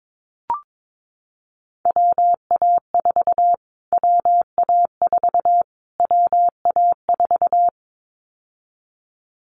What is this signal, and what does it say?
Morse code, heard as a keyed tone, about 22 wpm, 700 Hz. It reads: WA4 WA4 WA4
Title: Callsign Prefixes for Top 25 DXCC - Repeated 3x - 22wpm